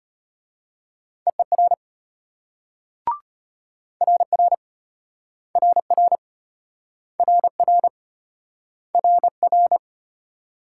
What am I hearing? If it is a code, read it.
Morse code, heard as a keyed tone, about 34 wpm, 700 Hz. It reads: EER RR RR RR RR